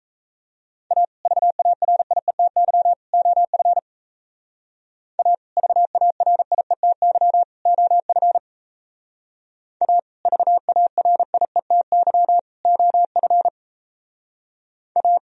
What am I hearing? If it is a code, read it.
Morse code, heard as a keyed tone, about 37 wpm, 700 Hz. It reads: A VARIETY OF A VARIETY OF A VARIETY OF A